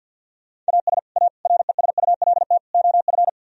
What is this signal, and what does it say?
Morse code, heard as a keyed tone, about 64 wpm, 700 Hz. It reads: AS A RESULT OF